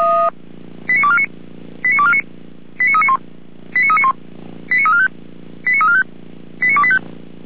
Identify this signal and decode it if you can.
Five-tone selective call (ZVEI), heard as quick runs of five tones, tones of about 80 ms. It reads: 89249 89249 89391 89391 89345 89345 89285